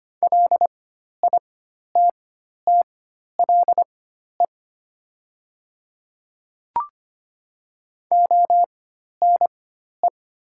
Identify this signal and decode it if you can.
Morse code, heard as a keyed tone, about 25 wpm, spaced as for 12 wpm, 700 Hz. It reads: LITTLE ONE
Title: 4x Character Spacing - Top 200 Words - 25wpm